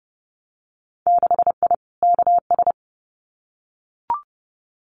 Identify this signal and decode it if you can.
Morse code, heard as a keyed tone, about 30 wpm, 700 Hz. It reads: 6I KS
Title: Winter Field Day Exchanges 30wpm